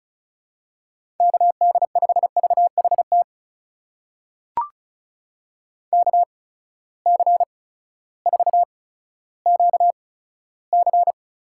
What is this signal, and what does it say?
Morse code, heard as a keyed tone, about 36 wpm, spaced as for 16 wpm, 700 Hz. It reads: KD5VHT KC4QC
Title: USA Call Signs - 6 Characters - 8x Character Spacing - 35wpm